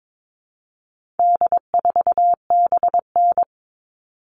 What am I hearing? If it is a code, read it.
Morse code, heard as a keyed tone, about 22 wpm, 700 Hz. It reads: D4BN
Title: All Cut Numbers - 4-Digit 22wpm